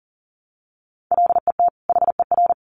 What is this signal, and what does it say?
Morse code, heard as a keyed tone, about 40 wpm, 700 Hz. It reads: LET HER
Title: Sentences from Top 500 Words 40wpm